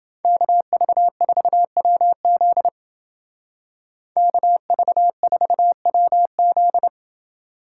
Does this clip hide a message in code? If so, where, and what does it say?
Yes: Morse code from 0.2 to 6.9 s, heard as a keyed tone, about 31 wpm, 700 Hz. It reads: KV4WZ KV4WZ